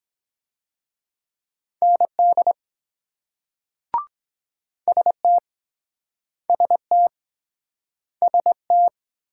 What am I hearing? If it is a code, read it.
Morse code, heard as a keyed tone, about 23 wpm, 700 Hz. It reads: ND ST ST ST